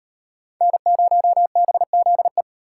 Morse code, heard as a keyed tone, about 38 wpm, 700 Hz. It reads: N0BZE